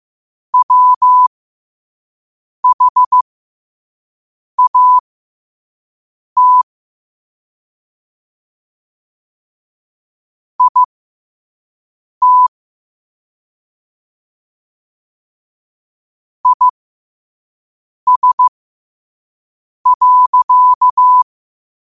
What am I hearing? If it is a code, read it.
Morse code, heard as a keyed tone, about 15 wpm, spaced as for 5 wpm, 1000 Hz. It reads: WHAT IT IS.
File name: qotdfiles / 5wpm